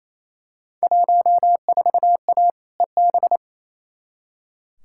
Morse code, heard as a keyed tone, about 28 wpm, 700 Hz. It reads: 14A EB
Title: Field Day Exchanges Encore - 28wpm